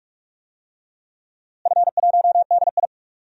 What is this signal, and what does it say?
Morse code, heard as a keyed tone, about 46 wpm, 700 Hz. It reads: F1DI
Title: DXCC-100 Call Signs - 4 Characters - Repeated 3x - 45wpm